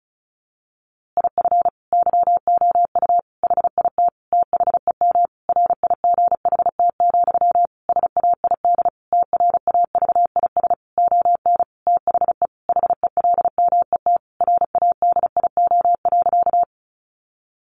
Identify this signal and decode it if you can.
Morse code, heard as a keyed tone, about 34 wpm, 700 Hz. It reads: IF YOU HIT THEM RIGHT, SAID TRAVIS ON THE HELMET RADIO.
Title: A Sound Of Thunder 35wpm